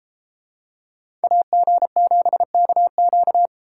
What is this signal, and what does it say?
Morse code, heard as a keyed tone, about 31 wpm, 700 Hz. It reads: AG7KQ